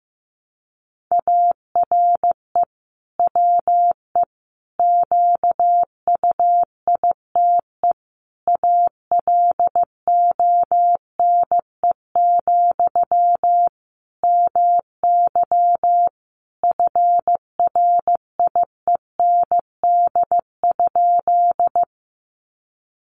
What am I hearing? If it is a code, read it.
Morse code, heard as a keyed tone, about 15 wpm, 700 Hz. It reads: ARE WE QUITE ALONE, MY FRIEND?
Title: Sentences from Top 1000 Words 15wpm